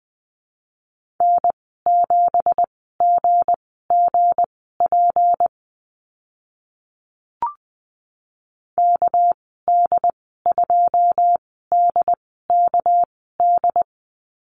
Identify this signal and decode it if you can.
Morse code, heard as a keyed tone, about 19 wpm, spaced as for 14 wpm, 700 Hz. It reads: N7GGP KD2DKD
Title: 219 Lesson 35g - Number 0 - Call Signs 20wpm